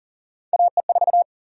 Morse code, heard as a keyed tone, about 40 wpm, 700 Hz. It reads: AE4
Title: USA Call Sign Prefixes - 40wpm